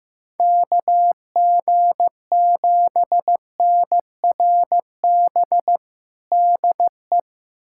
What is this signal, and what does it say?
Morse code, heard as a keyed tone, about 16 wpm, 700 Hz. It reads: KG7NRB DE
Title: USA Callsign DE Callsign 15wpm